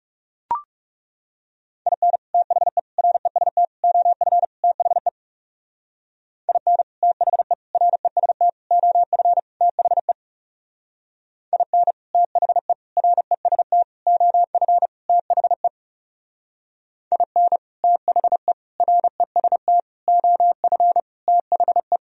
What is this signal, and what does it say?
Morse code, heard as a keyed tone, about 37 wpm, 700 Hz. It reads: IN THE REST OF THE IN THE REST OF THE IN THE REST OF THE IN THE REST OF THE
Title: Speed-Racing - Sets of 5 Words 30wpm